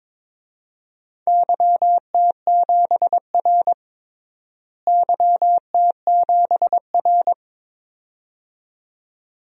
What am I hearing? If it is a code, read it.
Morse code, heard as a keyed tone, about 22 wpm, 700 Hz. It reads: YT7R YT7R
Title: DXCC-100 Call Signs - 4 Characters - Repeated 2x - 22wpm